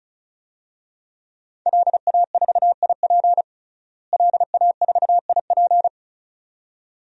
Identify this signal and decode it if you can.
Morse code, heard as a keyed tone, about 36 wpm, 700 Hz. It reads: LA4IP LA4IP
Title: DXCC-100 Call Signs - 5 Characters - Repeated 2x - 35wpm